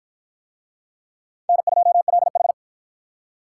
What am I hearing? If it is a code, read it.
Morse code, heard as a keyed tone, about 53 wpm, 700 Hz. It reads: N2LH